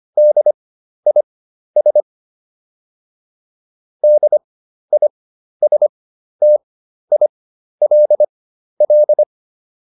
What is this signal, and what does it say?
Morse code, heard as a keyed tone, about 25 wpm, spaced as for 12 wpm, 600 Hz. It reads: DIS DISTILL